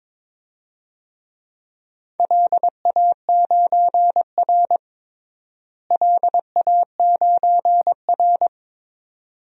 Morse code, heard as a keyed tone, about 22 wpm, 700 Hz. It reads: LA9R LA9R
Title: DXCC-100 Call Signs - 4 Characters - Repeated 2x - 22wpm